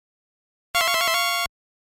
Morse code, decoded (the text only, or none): none